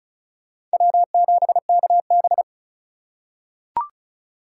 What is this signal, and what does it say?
Morse code, heard as a keyed tone, about 36 wpm, 700 Hz. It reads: W7KB
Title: USA Call Signs - 4 Characters - 3x Character Spacing - 35wpm